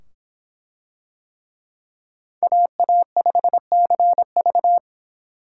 Morse code, heard as a keyed tone, about 25 wpm, 700 Hz. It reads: AA5CV